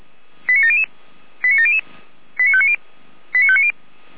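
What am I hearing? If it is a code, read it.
Five-tone selective call (ZVEI), heard as quick runs of five tones, tones of about 70 ms. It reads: 89700 89700 89590 89590
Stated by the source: ZVEI five-tone selective call received by SDR.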